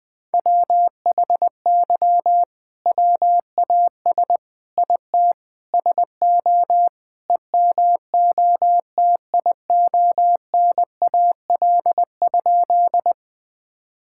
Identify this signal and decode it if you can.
Morse code, heard as a keyed tone, about 20 wpm, 700 Hz. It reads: WHY WAS IT SO EMOTIONAL?
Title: Sentences from Top 1000 Words - Review Part 5 - 20wpm